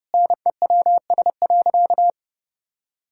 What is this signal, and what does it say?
Morse code, heard as a keyed tone, about 30 wpm, 700 Hz. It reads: NEWS.
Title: Sentences from Top 1000 Words - Review Part 6 - 30wpm